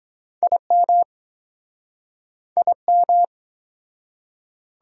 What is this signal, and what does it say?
Morse code, heard as a keyed tone, about 21 wpm, 700 Hz. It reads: IM IM
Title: Speed-Racing Prefixes and Suffixes 17wpm